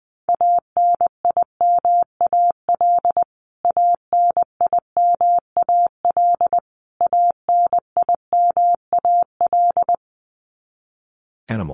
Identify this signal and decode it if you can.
Morse code, heard as a keyed tone, about 20 wpm, 700 Hz. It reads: ANIMAL ANIMAL ANIMAL
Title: Top 200 Words Repeated 20wpm